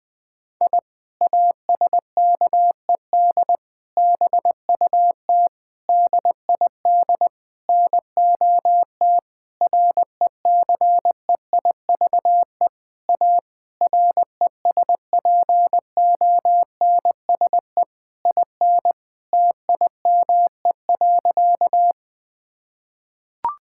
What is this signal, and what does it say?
Morse code, heard as a keyed tone, about 21 wpm, 700 Hz. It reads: I ASKED BUT DID NOT RECEIVE A RESPONSE IN TIME.
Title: Sentences with 8-Word Phrases - Spoken Beginning - 20wpm